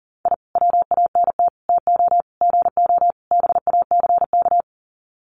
Morse code, heard as a keyed tone, about 40 wpm, 700 Hz. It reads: I WANT TO GO BACK
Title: Sets of 5 Words Encore 40wpm